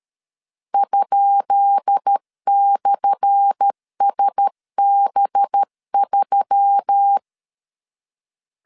Morse code, heard as a keyed tone, about 13 wpm, 800 Hz. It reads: ?/SB3
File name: LEZIONI / 10 SIGNS / 10B INT 12wpm 2414